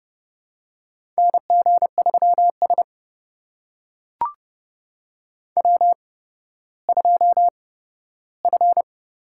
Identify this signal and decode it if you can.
Morse code, heard as a keyed tone, about 30 wpm, spaced as for 15 wpm, 700 Hz. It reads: NG3S W2F
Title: USA Call Signs - 4 Characters - 8x Character Spacing - 30wpm